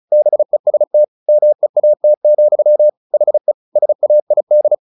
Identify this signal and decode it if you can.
Morse code, heard as a keyed tone, about 34 wpm, 600 Hz. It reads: BEST MEAT, HE SAID
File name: the old man and the sea 2 35wpm